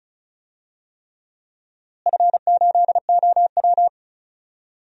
Morse code, heard as a keyed tone, about 36 wpm, 700 Hz. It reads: F8OW
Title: DXCC-100 Call Signs - 4 Characters - Repeated 3x - 35wpm